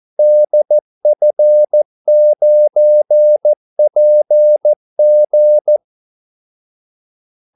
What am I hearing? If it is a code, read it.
Morse code, heard as a keyed tone, about 14 wpm, 600 Hz. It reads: DF9PG
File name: Callsigns-14wpm